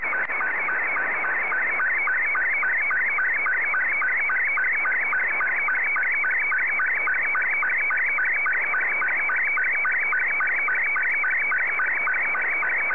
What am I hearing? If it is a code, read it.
SSTV (slow-scan television), Scottie 2.